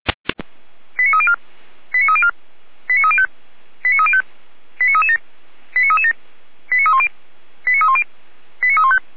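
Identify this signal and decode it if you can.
Five-tone selective call (ZVEI), heard as quick runs of five tones, tones of about 80 ms. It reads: 89304 89304 89305 89305 89337 89337 89310 89310 89315
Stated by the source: ZVEI five-tone selective call received via SDR, recorded by Feuerwehr Orlamünde fire brigade.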